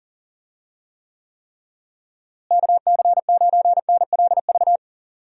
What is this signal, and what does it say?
Morse code, heard as a keyed tone, about 40 wpm, 700 Hz. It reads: KC9NLV
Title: USA Call Signs - 6 Characters - Repeated 3x - 40wpm